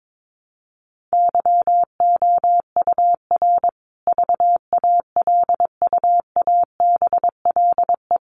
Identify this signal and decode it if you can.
Morse code, heard as a keyed tone, about 22 wpm, 700 Hz. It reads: YOUR VALUABLE
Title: Sentences from Top 1300 Words 22wpm